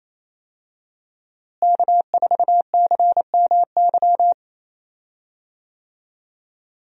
Morse code, heard as a keyed tone, about 28 wpm, 700 Hz. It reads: K4CMY
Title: USA Call Signs - 5 Characters - Repeated 2x - 28wpm